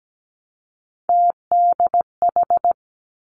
Morse code, heard as a keyed tone, about 17 wpm, 700 Hz. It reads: TDH
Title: Three Challenging Letter-Number Combinations 17wpm